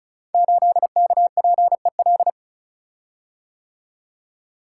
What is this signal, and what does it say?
Morse code, heard as a keyed tone, about 36 wpm, 700 Hz. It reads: 8KPEL
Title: Code Groups - 5 Letter:Numbers - 35wpm